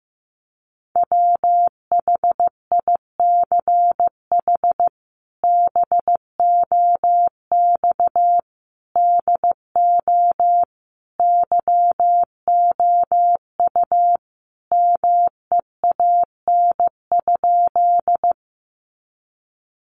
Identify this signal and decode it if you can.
Morse code, heard as a keyed tone, about 15 wpm, 700 Hz. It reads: WHICH BOX DO YOU MEAN?
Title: Sentences from Top 500 Words 15wpm